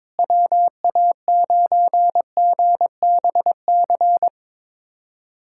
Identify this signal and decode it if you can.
Morse code, heard as a keyed tone, about 22 wpm, 700 Hz. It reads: WA9GBC
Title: USA Callsign DE Callsign - 6 Characters - 22wpm